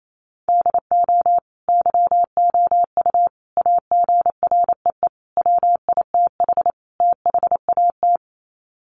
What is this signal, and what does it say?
Morse code, heard as a keyed tone, about 28 wpm, 700 Hz. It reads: DO YOU AGREE WITH THAT
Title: Sets of 5 Words Encore 28wpm